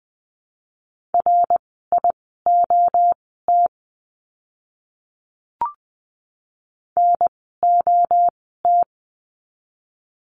Morse code, heard as a keyed tone, about 20 wpm, spaced as for 14 wpm, 700 Hz. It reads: RIOT NOT
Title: 061 Lesson 11f - Letter h - Words 20wpm